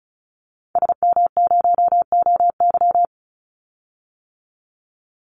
Morse code, heard as a keyed tone, about 34 wpm, 700 Hz. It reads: SM0OY
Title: Call Signs - 35wpm